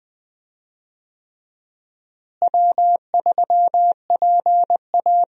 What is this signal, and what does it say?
Morse code, heard as a keyed tone, about 21 wpm, 700 Hz. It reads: W3PA
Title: USA Call Signs - 4 Characters - Repeated 3x - 20wpm